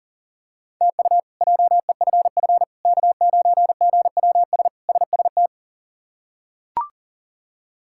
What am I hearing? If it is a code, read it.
Morse code, heard as a keyed tone, about 37 wpm, 700 Hz. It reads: TU JEFF K9GWS SST
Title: K1USN SST QSO Practice - 40wpm